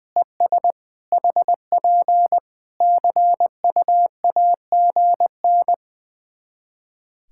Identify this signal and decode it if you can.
Morse code, heard as a keyed tone, about 20 wpm, 700 Hz. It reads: ES HP CUAGN
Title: Practice QSO - Exchange 4 - Ending - 20wpm